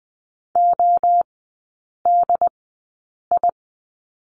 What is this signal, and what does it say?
Morse code, heard as a keyed tone, about 20 wpm, spaced as for 8 wpm, 700 Hz. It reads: ODI
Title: 067 Lesson 12e - Letter d - Three Characters 20wpm